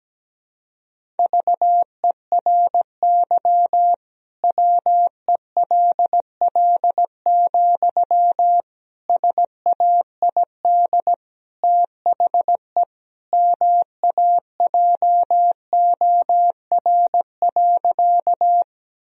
Morse code, heard as a keyed tone, about 17 wpm, 700 Hz. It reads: VERY WELL, SAID THE MAJOR.
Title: Sentences from Top 1000 Words - Review Part 4 - 17wpm